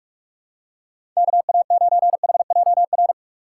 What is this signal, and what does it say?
Morse code, heard as a keyed tone, about 42 wpm, 700 Hz. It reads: KA9HJR
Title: USA Call Signs - 6 Characters - 3x Character Spacing - 45wpm